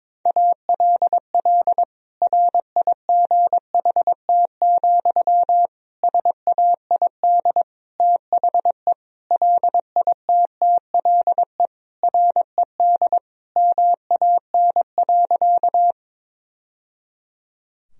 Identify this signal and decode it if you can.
Morse code, heard as a keyed tone, about 22 wpm, 700 Hz. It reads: ALL RIGHT, SAID THE LITTLE RED MAN.
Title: Sentences from Top 1000 Words - Review Part 2 - 22wpm